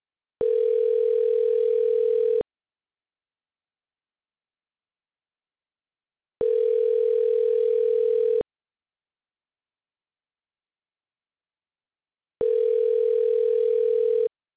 Ringback tone (North America), 440 + 480 Hz.